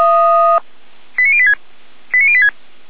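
Five-tone selective call (ZVEI), heard as quick runs of five tones, tones of about 70 ms. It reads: 89086 89086
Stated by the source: ZVEI five-tone call received by SDR, from the Feuerwehr Orlamünde server.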